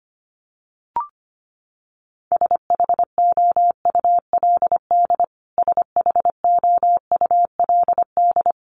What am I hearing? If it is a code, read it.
Morse code, heard as a keyed tone, about 25 wpm, 700 Hz. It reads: SHOULD SHOULD
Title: Top 300 Words Repeated 25wpm